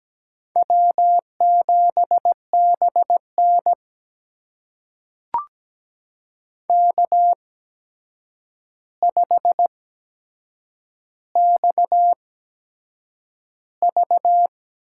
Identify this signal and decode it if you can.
Morse code, heard as a keyed tone, about 17 wpm, spaced as for 7 wpm, 700 Hz. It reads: W7BN K5XV
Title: USA Call Signs - 8x Character Spacing - 17wpm